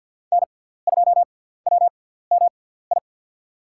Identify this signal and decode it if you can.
Morse code, heard as a keyed tone, about 50 wpm, spaced as for 17 wpm, 700 Hz. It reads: N2WMI
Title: USA Call Signs - 5 Characters - 6x Character Spacing - 50wpm